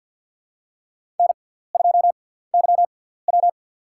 Morse code, heard as a keyed tone, about 50 wpm, spaced as for 17 wpm, 700 Hz. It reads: N2YW
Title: USA Call Signs - 6x Character Spacing - 50wpm